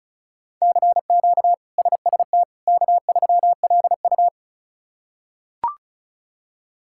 Morse code, heard as a keyed tone, about 34 wpm, 700 Hz. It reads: CQ SST K3LU
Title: K1USN SST QSO Practice - 35wpm